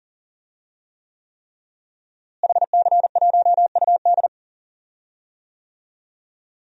Morse code, heard as a keyed tone, about 40 wpm, 700 Hz. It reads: HC1UD
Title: DXCC-100 Call Signs - 5 Characters - Repeated 3x - 40wpm